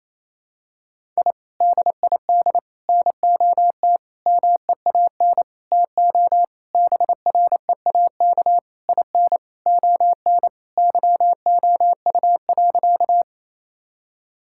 Morse code, heard as a keyed tone, about 28 wpm, 700 Hz. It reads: I DID NOT MEAN TO BREAK IN ON YOU.
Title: Sentences from Top 1000 Words - Review Part 6 - 28wpm